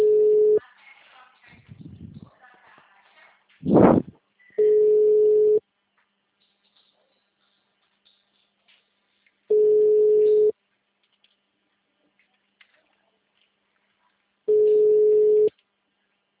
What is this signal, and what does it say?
Ringback tone (Europe, 425 Hz), with speech.
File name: out- -2015- - - .566